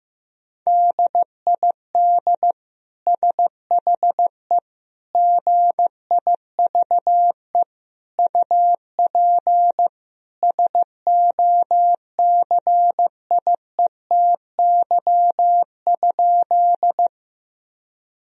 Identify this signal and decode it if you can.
Morse code, heard as a keyed tone, about 15 wpm, 700 Hz. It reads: DID SHE GIVE UP SOCIETY?
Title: Sentences from Top 1000 Words - Review Part 4 - 15wpm